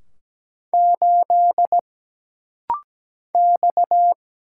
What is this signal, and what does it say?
Morse code, heard as a keyed tone, about 17 wpm, 700 Hz. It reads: 8 X
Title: Single Letter-Number - Rapid-Fire 17wpm